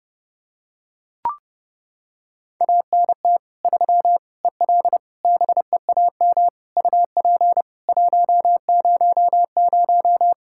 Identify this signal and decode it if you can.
Morse code, heard as a keyed tone, about 30 wpm, 700 Hz. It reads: ANT 3 EL BEAM UP 100
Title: Practice QSO - Exchange 2 - Info - 30wpm